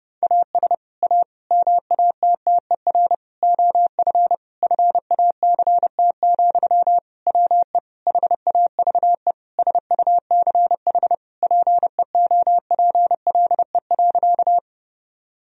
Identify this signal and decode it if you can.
Morse code, heard as a keyed tone, about 30 wpm, 700 Hz. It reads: AS A MATTER OF FACT, WE HAVE SUCH PEOPLE.
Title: Sentences from Top 1000 Words - Review Part 3 - 30wpm